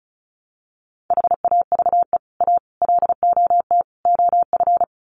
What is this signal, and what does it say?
Morse code, heard as a keyed tone, about 34 wpm, 700 Hz. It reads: HAVE A LOT OF
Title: Sets of 4 Words 35wpm